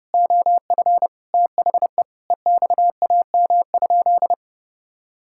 Morse code, heard as a keyed tone, about 30 wpm, 700 Hz. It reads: OF THE EXAM?
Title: Sentences from Top 1000 Words - Review Part 5 - 30wpm